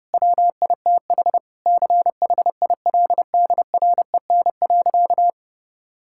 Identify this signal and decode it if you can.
Morse code, heard as a keyed tone, about 32 wpm, 700 Hz. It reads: WITH CHILDREN.
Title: Sentences from Top 1000 Words - Review Part 5 - 30wpm